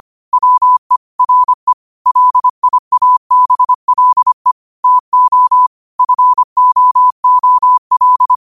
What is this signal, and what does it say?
Morse code, heard as a keyed tone, about 25 wpm, 1000 Hz. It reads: WE RE LIABLE TO FOOL
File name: qotdfiles / 25wpm